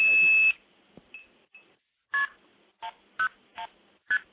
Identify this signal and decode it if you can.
Multi-frequency trunk signalling, heard as fast pairs of tones, tones of about 90 ms. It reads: KP101ST